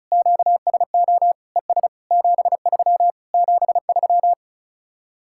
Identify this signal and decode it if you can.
Morse code, heard as a keyed tone, about 33 wpm, 700 Hz. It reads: QSO ES 73 73
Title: Speed-Racing Common QSO Phrases - 35wpm